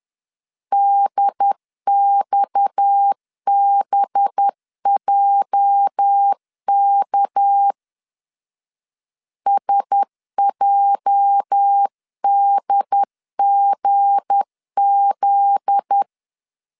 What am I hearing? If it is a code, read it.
Morse code, heard as a keyed tone, about 10 wpm, 800 Hz. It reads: DXBJK SJDGZ
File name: LEZIONI / 8 YJ45 / 8 YJ 10wpm 2651